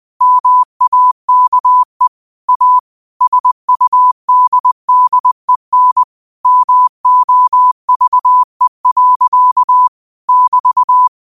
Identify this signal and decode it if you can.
Morse code, heard as a keyed tone, about 20 wpm, 1000 Hz. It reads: MAKE A SUDDEN MOVE. <BT>
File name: qotdfiles / 20wpm